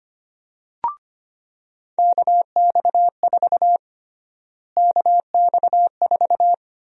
Morse code, heard as a keyed tone, about 25 wpm, 700 Hz. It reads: KX4 KX4
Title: USA Call Sign Prefixes - Repeated 2x - 25wpm